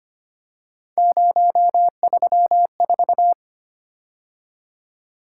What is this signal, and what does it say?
Morse code, heard as a keyed tone, about 25 wpm, 700 Hz. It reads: 034
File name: Three Letter-Number Combinations - 25wpm